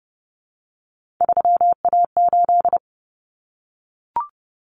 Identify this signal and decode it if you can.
Morse code, heard as a keyed tone, about 30 wpm, 700 Hz. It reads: 3A8
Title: Common Cut Numbers - 3-Digit 30wpm